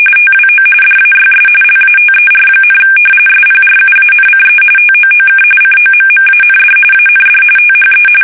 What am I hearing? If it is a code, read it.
RTTY, about 45.45 baud. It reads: TWO DAYS LATER, SPRUANCE TOOK COMMAND AND THE
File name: fsk850 / battleshipnj